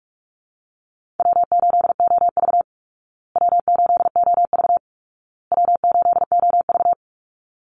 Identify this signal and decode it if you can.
Morse code, heard as a keyed tone, about 44 wpm, 700 Hz. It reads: W8OV W8OV W8OV
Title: Easy Call Signs Repeated 3x - 45wpm